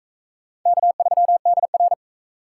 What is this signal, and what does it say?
Morse code, heard as a keyed tone, about 39 wpm, 700 Hz. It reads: K3DR